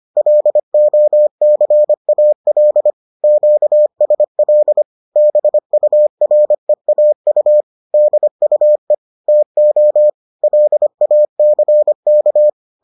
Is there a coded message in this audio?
Yes: Morse code from 0.2 to 12.5 s, heard as a keyed tone, about 25 wpm, 600 Hz. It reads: LOCAL QSL BUREAU DUE TO LACK